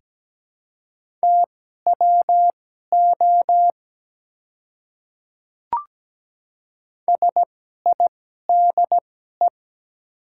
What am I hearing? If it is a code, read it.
Morse code, heard as a keyed tone, about 17 wpm, spaced as for 12 wpm, 700 Hz. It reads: TWO SIDE
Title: 2x Character Spacing - Top 100 Words - 17wpm